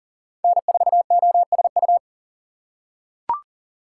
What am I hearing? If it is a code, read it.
Morse code, heard as a keyed tone, about 40 wpm, 700 Hz. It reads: N4OSU
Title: USA Call Signs - 5 Characters - 3x Character Spacing - 40wpm